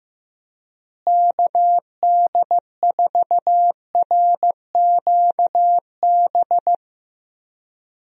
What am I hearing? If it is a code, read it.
Morse code, heard as a keyed tone, about 15 wpm, 700 Hz. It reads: KD4RQB
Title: USA Call Signs - 6 Characters - Repeated 3x - 15wpm